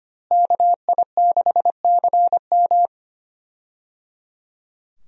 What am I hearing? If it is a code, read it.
Morse code, heard as a keyed tone, about 25 wpm, 700 Hz. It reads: KI6CM
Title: USA Call Signs - 5 Characters - Repeated 3x - 25wpm